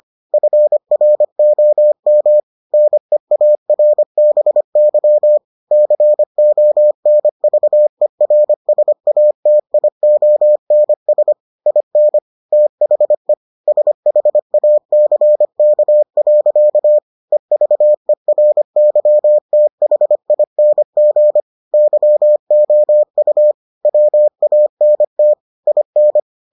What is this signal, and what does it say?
Morse code, heard as a keyed tone, about 25 wpm, 600 Hz. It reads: FROM NEARBY CONVERSATIONS IN THE SHACK. EVERYTHING YOU WANT IN